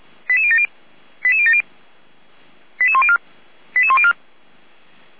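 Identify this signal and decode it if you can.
Five-tone selective call (ZVEI), heard as quick runs of five tones, tones of about 70 ms. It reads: 88070 88070 88104 88104
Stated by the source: Five-tone ZVEI selective call received on SDR.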